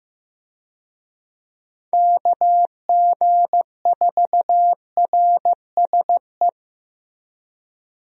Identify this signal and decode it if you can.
Morse code, heard as a keyed tone, about 15 wpm, 700 Hz. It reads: KG4RSE
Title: USA Call Signs - 6 Characters - Repeated 2x - 15wpm